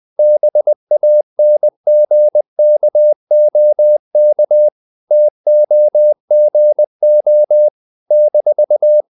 Morse code, heard as a keyed tone, about 20 wpm, 600 Hz. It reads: BANGKOK TOGO -
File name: CW / Practice / NationalCapitals20